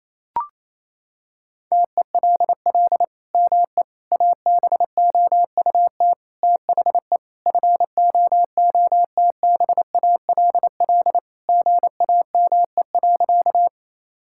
Morse code, heard as a keyed tone, about 28 wpm, 700 Hz. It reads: TELL ME ABOUT THE FOOTBALL GAME.
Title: Sentences from Top 1600 Words 28wpm